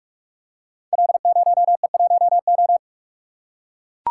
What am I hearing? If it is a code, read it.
Morse code, heard as a keyed tone, about 44 wpm, 700 Hz. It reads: L0E1O
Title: Code Groups - 5 Letter:Numbers - 45wpm